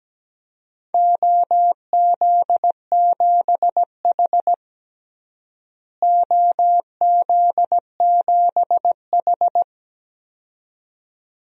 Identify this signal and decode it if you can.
Morse code, heard as a keyed tone, about 17 wpm, 700 Hz. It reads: OZ7H OZ7H
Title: DXCC-100 Call Signs - 4 Characters - Repeated 2x - 17wpm